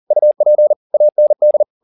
Morse code, heard as a keyed tone, about 40 wpm, 600 Hz. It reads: UP AND
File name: the old man and the sea 2 40wpm